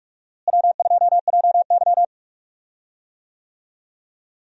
Morse code, heard as a keyed tone, about 41 wpm, 700 Hz. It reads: W2JY
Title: USA Call Signs - 4 Characters - Repeated 3x - 45wpm